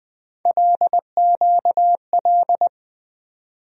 Morse code, heard as a keyed tone, about 20 wpm, 700 Hz. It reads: LQL
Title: USA Call Sign Postfixes - 3 Letters 20wpm